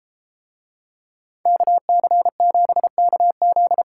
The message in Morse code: KC7KZ

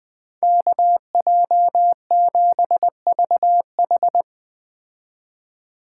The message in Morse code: KJ7VH

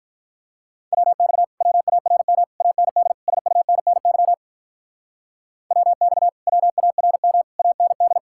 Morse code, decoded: WX WARM AND SUNNY WX WARM AND

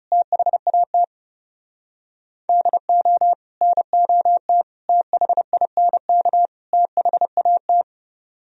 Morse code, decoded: THAT DO NOT THINK THAT

